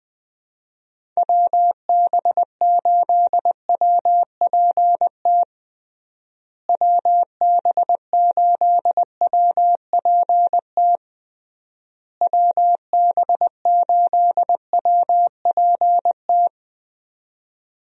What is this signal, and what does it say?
Morse code, heard as a keyed tone, about 21 wpm, 700 Hz. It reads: WB8WPT WB8WPT WB8WPT
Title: USA Call Signs - 6 Characters - Repeated 3x - 20wpm